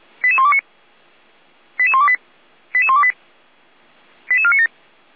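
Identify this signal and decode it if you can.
Five-tone selective call (ZVEI), heard as quick runs of five tones, tones of about 70 ms. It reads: 88129 88128 88128 88497
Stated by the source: ZVEI five-tone selective call received on an SDR.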